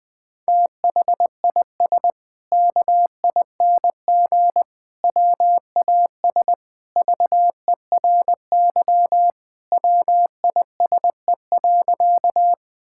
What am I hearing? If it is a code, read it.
Morse code, heard as a keyed tone, about 20 wpm, 700 Hz. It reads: THIS KING WAS VERY WISE.
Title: Sentences from Top 1600 Words 20wpm